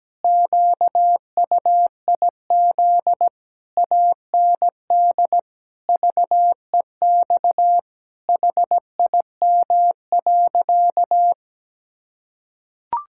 Morse code, heard as a keyed tone, about 17 wpm, 700 Hz. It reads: QUIZ AND VEX HIM.
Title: Pangram Challenge v2 - 17wpm